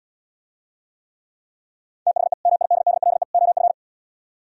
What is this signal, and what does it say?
Morse code, heard as a keyed tone, about 77 wpm, 700 Hz. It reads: THE MIDDLE OF